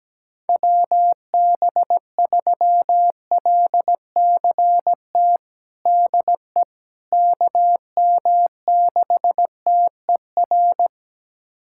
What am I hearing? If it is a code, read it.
Morse code, heard as a keyed tone, about 17 wpm, 700 Hz. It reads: WB3LCT DE KM6TER
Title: USA Callsign DE Callsign 17wpm